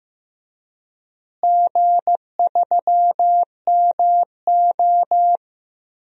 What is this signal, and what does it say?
Morse code, heard as a keyed tone, about 15 wpm, 700 Hz. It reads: G3MO
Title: DXCC-100 Call Signs - 4 Characters - Repeated 3x - 15wpm